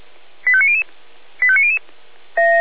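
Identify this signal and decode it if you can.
Five-tone selective call (ZVEI), heard as quick runs of five tones, tones of about 70 ms. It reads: 85900 85900
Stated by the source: ZVEI five-tone call 85300 received by SDR, from the feuerwehr-orlamuende.org fire brigade audio archive.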